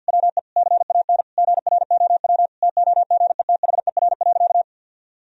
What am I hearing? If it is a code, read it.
Morse code, heard as a keyed tone, about 52 wpm, 700 Hz. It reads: WE CAN GROW TOGETHER.
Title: Sentences from Top 1000 Words - Review Part 1 - 50wpm